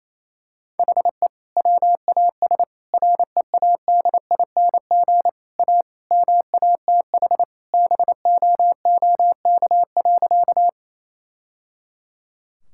Morse code, heard as a keyed tone, about 28 wpm, 700 Hz. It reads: HE WAS READING A MATH BOOK.
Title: Sentences from Top 1000 Words - Review Part 6 - 28wpm